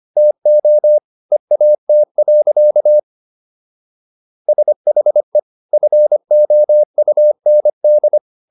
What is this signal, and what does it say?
Morse code, heard as a keyed tone, about 25 wpm, 600 Hz. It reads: TO EAT. SHE FOUND